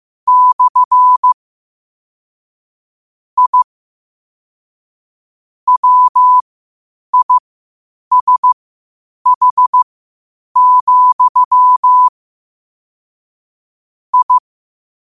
Morse code, heard as a keyed tone, about 15 wpm, spaced as for 8 wpm, 1000 Hz. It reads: / I WISH, I
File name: qotdfiles / 8wpm